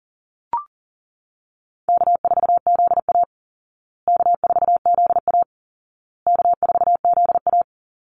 Morse code, heard as a keyed tone, about 40 wpm, 700 Hz. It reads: K4ZA K4ZA K4ZA
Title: Easy Call Signs Repeated 3x - 40wpm